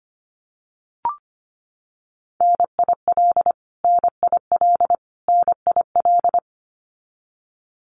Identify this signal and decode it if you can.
Morse code, heard as a keyed tone, about 25 wpm, 700 Hz. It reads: NIL NIL NIL
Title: Abbreviations Repeated 3x - 25wpm